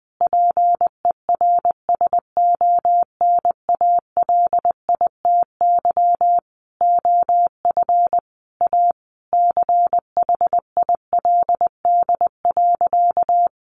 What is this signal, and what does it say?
Morse code, heard as a keyed tone, about 20 wpm, 700 Hz. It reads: PERSONALITY OF A CHILD.
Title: Sentences from Top 1400 Words 20wpm